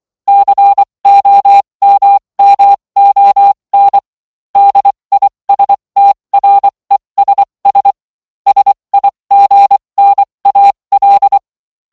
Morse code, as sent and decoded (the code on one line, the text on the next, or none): -.-. --- -- -- --- -. / -.. .. ... - .-. ... ... / ... .. --. -. .- .-..
COMMON DISTRESS SIGNAL